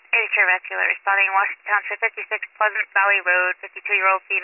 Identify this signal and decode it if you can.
Emergency Alert System (SAME) tones, heard as harsh buzzing bursts, with speech. It is heard from 0.3 to 3.6 s.